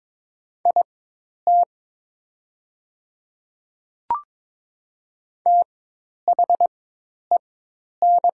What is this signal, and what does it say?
Morse code, heard as a keyed tone, about 22 wpm, spaced as for 10 wpm, 700 Hz. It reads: IT THEN
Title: 4x Character Spacing - Top 100 Words - 22wpm